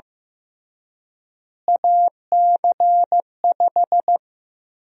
Morse code, heard as a keyed tone, about 15 wpm, 700 Hz. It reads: AC5
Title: USA Call Sign Prefixes - Repeated 2x - 15wpm